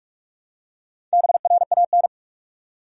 Morse code, heard as a keyed tone, about 46 wpm, 700 Hz. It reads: BRAN
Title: USA Common Names - Max 4 Letters - 45wpm